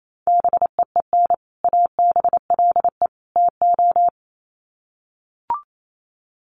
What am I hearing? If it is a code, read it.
Morse code, heard as a keyed tone, about 28 wpm, 700 Hz. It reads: BEEN ABLE TO
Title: Sets of 3 Words 28wpm